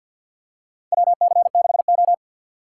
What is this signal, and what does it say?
Morse code, heard as a keyed tone, about 51 wpm, 700 Hz. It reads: WC6O
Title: USA Call Signs - 4 Characters - 6x Character Spacing - 50wpm